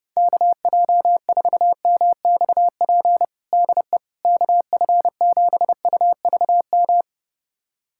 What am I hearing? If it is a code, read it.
Morse code, heard as a keyed tone, about 32 wpm, 700 Hz. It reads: KJ4MXP DE KF7UVM
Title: USA Callsign DE Callsign 30wpm